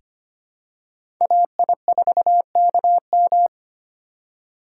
Morse code, heard as a keyed tone, about 25 wpm, 700 Hz. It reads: AI4KM
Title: USA Call Signs - 3x Character Spacing - 25wpm